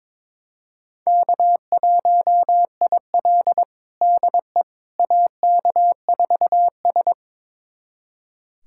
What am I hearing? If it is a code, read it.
Morse code, heard as a keyed tone, about 22 wpm, 700 Hz. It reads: K1IL DE AK4S
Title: USA Callsign DE Callsign - 4 Characters - 22wpm